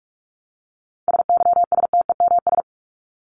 Morse code, heard as a keyed tone, about 44 wpm, 700 Hz. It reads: SYSTEMS
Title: ES:S Suffix - Top 100 Words with Base Words 45wpm